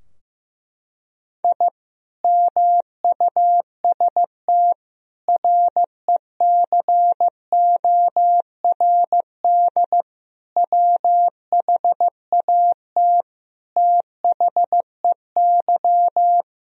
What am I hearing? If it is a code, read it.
Morse code, heard as a keyed tone, about 15 wpm, 700 Hz. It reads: I MUST RECORD WHAT THEY
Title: Sentences from Top 1000 Words - Review Part 3 - 15wpm